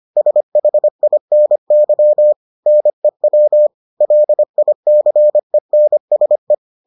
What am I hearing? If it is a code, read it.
Morse code, heard as a keyed tone, about 25 wpm, 600 Hz. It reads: SHINY NEW LICENSE